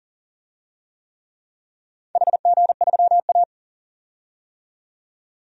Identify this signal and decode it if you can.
Morse code, heard as a keyed tone, about 40 wpm, 700 Hz. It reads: HG3A
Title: DXCC-100 Call Signs - 4 Characters - 40wpm